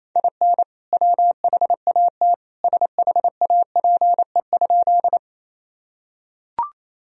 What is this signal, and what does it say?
Morse code, heard as a keyed tone, about 28 wpm, 700 Hz. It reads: IN WHAT SHAPE?
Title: Sentences from Top 1000 Words - Review Part 3 - 28wpm